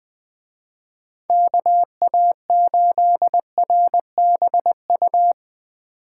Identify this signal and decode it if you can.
Morse code, heard as a keyed tone, about 20 wpm, 700 Hz. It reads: KA8RBU